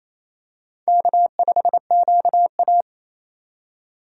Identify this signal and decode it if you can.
Morse code, heard as a keyed tone, about 28 wpm, 700 Hz. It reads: K5QA
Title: USA Call Signs - 4 Characters - 2x Character Spacing - 28wpm